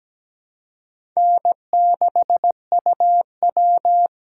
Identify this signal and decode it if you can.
Morse code, heard as a keyed tone, about 17 wpm, 700 Hz. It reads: N6UW
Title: USA Call Signs - 4 Characters - 2x Character Spacing - 17wpm